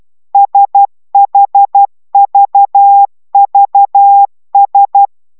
Morse code, heard as a keyed tone, about 12 wpm, 800 Hz. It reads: SHVVS
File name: taviro tanulas / cw 69 04 10